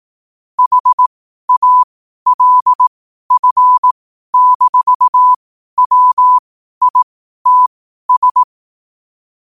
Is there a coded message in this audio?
Yes: Morse code from 0.6 to 8.4 s, heard as a keyed tone, about 18 wpm, spaced as for 12 wpm, 1000 Hz. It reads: HALF-WITS